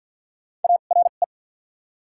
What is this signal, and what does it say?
Morse code, heard as a keyed tone, about 50 wpm, spaced as for 36 wpm, 690 Hz. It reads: ARE
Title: 2x Character Spacing - Top 100 Words - 50wpm